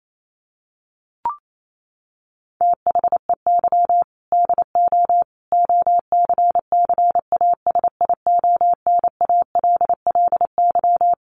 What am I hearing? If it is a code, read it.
Morse code, heard as a keyed tone, about 28 wpm, 700 Hz. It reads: THEY DO OCCASIONALLY
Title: Sentences from Top 1300 Words 28wpm